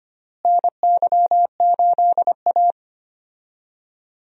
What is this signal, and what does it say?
Morse code, heard as a keyed tone, about 25 wpm, 700 Hz. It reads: NY8A
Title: USA Call Signs - 4 Characters - Repeated 3x - 25wpm